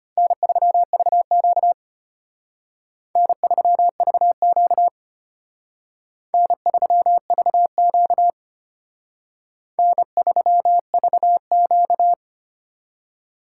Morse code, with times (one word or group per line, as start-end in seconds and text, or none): 0.2-1.7 N3VQ
3.2-4.9 N3VQ
6.3-8.3 N3VQ
9.8-12.1 N3VQ